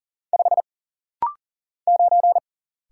Morse code, heard as a keyed tone, about 40 wpm, 700 Hz. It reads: 5 9
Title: Single Letter-Number - Rapid-Fire 40wpm